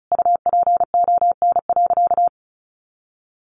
Morse code, heard as a keyed tone, about 34 wpm, 700 Hz. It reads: UPON.
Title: Sentences from Top 1400 Words 35wpm